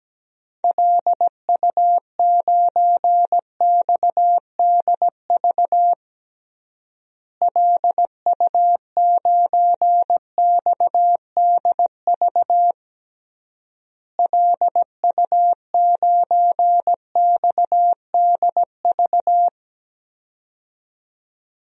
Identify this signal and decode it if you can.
Morse code, heard as a keyed tone, about 17 wpm, 700 Hz. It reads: LU9XDV LU9XDV LU9XDV
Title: DXCC-100 Call Signs - 6 Characters - Repeated 3x - 17wpm